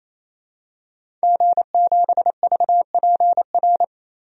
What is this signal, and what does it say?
Morse code, heard as a keyed tone, about 28 wpm, 700 Hz. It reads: G7VPR
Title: DXCC-100 Call Signs - 5 Characters - 28wpm